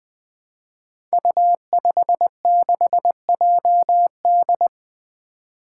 Morse code, heard as a keyed tone, about 20 wpm, 700 Hz. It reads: U56JD
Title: Five Challenging Letter-Number Combinations 20wpm